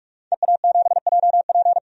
Morse code, heard as a keyed tone, about 41 wpm, 700 Hz. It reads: EA7JP
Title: DXCC-100 Call Signs - 5 Characters - 45wpm